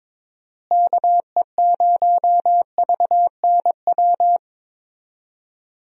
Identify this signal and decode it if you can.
Morse code, heard as a keyed tone, about 22 wpm, 700 Hz. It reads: KE0VNW
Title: USA Call Signs - 6 Characters - Repeated 3x - 22wpm